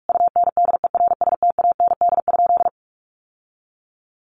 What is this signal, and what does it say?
Morse code, heard as a keyed tone, about 44 wpm, 700 Hz. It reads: UNDERSTAND?
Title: Sentences from Top 1100 Words 45wpm